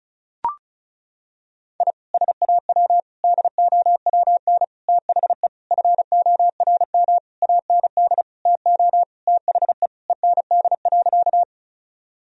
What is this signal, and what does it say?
Morse code, heard as a keyed tone, about 34 wpm, 700 Hz. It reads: I SAW DOWN THE FORM AND TO THE END.
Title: Sentences from Top 1000 Words - Review Part 1 - 35wpm